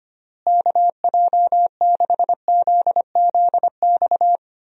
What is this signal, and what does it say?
Morse code, heard as a keyed tone, about 25 wpm, 700 Hz. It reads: KJ6ZZX
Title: USA Call Signs - 6 Characters - Repeated 2x - 25wpm